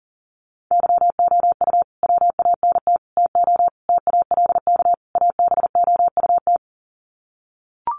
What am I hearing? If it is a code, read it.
Morse code, heard as a keyed tone, about 37 wpm, 700 Hz. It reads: YOU WANT TO TALK ABOUT
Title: Sets of 5 Words Encore 40wpm